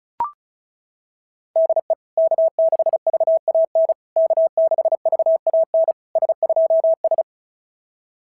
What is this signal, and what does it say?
Morse code, heard as a keyed tone, about 36 wpm, 660 Hz. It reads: DE K6VAN K6VAN S2S
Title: SOTA QSO Practice 35wpm